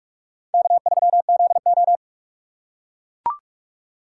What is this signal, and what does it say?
Morse code, heard as a keyed tone, about 44 wpm, 700 Hz. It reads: K3ZO